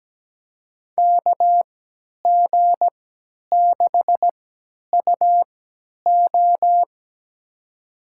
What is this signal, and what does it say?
Morse code, heard as a keyed tone, about 17 wpm, spaced as for 10 wpm, 700 Hz. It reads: KG6UO
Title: USA Call Signs - 3x Character Spacing - 17wpm